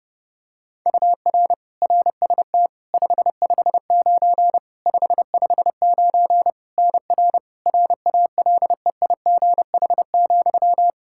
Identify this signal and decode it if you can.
Morse code, heard as a keyed tone, about 30 wpm, 700 Hz. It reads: UR RST 559 559 NR RALEIGH,
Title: K1USN SST QSO Practice 30wpm